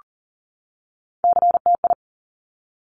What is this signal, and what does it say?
Morse code, heard as a keyed tone, about 42 wpm, 700 Hz. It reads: CTI
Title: Top Three Letter Combinations in Words 40wpm